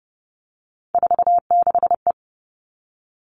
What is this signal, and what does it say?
Morse code, heard as a keyed tone, about 30 wpm, 700 Hz. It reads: 46E
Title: All Cut Numbers - 3-Digit 30wpm